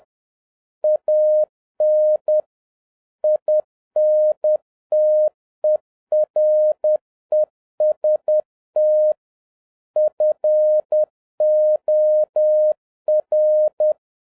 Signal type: Morse code